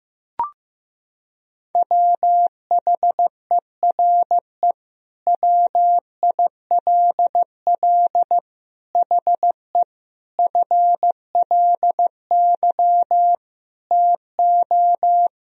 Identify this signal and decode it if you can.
Morse code, heard as a keyed tone, about 15 wpm, 700 Hz. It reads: WHERE WILL HE FLY TO
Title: Sentences from Top 1000 Words - Review Part 2 - 15wpm